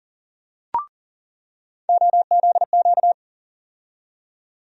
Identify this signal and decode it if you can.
Morse code, heard as a keyed tone, about 38 wpm, 700 Hz. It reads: OZQ
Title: USA Call Sign Postfixes - 3 Letters 40wpm